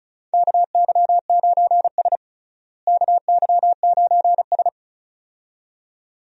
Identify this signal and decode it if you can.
Morse code, heard as a keyed tone, about 34 wpm, 700 Hz. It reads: KY9S KY9S
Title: USA Call Signs - 4 Characters - Repeated 2x - 35wpm